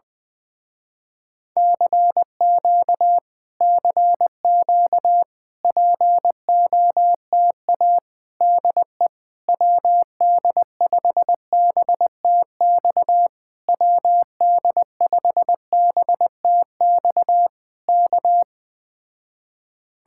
Morse code, heard as a keyed tone, about 20 wpm, 700 Hz. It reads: CQ CQ POTA DE WD5BTX WD5BTX K
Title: Speed-Racing Common QSO Phrases - 17wpm